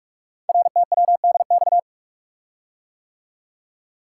Morse code, heard as a keyed tone, about 44 wpm, 700 Hz. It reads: RTWDX
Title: Code Groups - 5 Letters - 45wpm